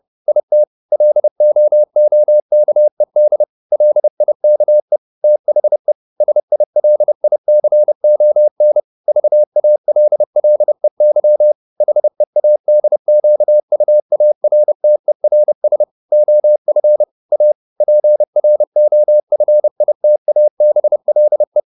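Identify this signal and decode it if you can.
Morse code, heard as a keyed tone, about 30 wpm, 600 Hz. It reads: IT LOOKED LIKE THE SILICON VALLEY HEADQUARTERS OF A PROFITABLE
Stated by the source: Morse code at 29 wpm, sending Ken Follett's novel Never in English, from CW med Gnister.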